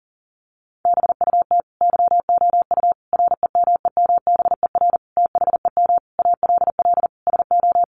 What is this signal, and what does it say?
Morse code, heard as a keyed tone, about 40 wpm, 700 Hz. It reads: BUT YOU REMEMBER THEM ALL SO
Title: Sentences from Top 1300 Words 40wpm